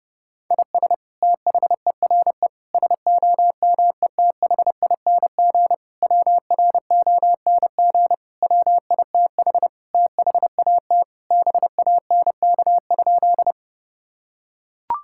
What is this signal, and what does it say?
Morse code, heard as a keyed tone, about 30 wpm, 700 Hz. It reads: IS THERE SOMETHING WRONG WITH THAT BANK?
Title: Sentences from Top 1000 Words - Review Part 6 - 30wpm